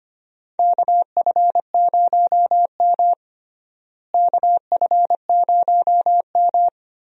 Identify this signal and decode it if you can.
Morse code, heard as a keyed tone, about 25 wpm, 700 Hz. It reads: KF0M KF0M
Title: USA Call Signs - 4 Characters - Repeated 2x - 25wpm